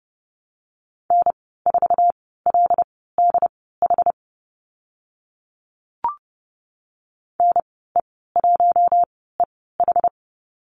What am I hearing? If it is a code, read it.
Morse code, heard as a keyed tone, about 30 wpm, spaced as for 17 wpm, 700 Hz. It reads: N4LDH NE1EH
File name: Lesson 14g - Number 2 - Call Signs 30wpm